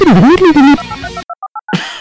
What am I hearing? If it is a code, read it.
DTMF, heard as telephone touch tones, tones of about 60 ms. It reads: 2822489